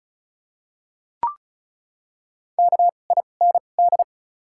Morse code, heard as a keyed tone, about 34 wpm, spaced as for 25 wpm, 700 Hz. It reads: KIND
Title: 2x Character Spacing - Top 200 Words - 35wpm